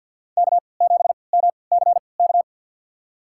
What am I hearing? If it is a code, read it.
Morse code, heard as a keyed tone, about 53 wpm, spaced as for 29 wpm, 700 Hz. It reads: K7MCK
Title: USA Call Signs - 5 Characters - 3x Character Spacing - 50wpm